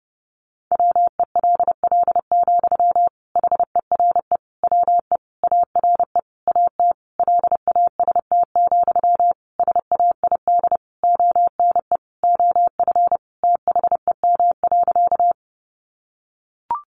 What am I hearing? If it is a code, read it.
Morse code, heard as a keyed tone, about 30 wpm, 700 Hz. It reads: WELL, HERE WE ARE AT LAST, SAID ONE OF THEM.
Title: Sentences from Top 300 Words 30wpm